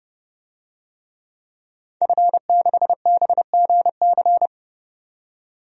Morse code, heard as a keyed tone, about 32 wpm, 700 Hz. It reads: F6BGC
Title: DXCC-100 Call Signs - 5 Characters - 30wpm